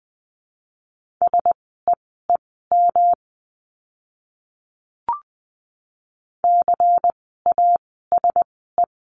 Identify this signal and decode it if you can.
Morse code, heard as a keyed tone, about 20 wpm, spaced as for 15 wpm, 700 Hz. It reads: SEEM CASE
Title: 206 Lesson 33f - Letter j - Words 20wpm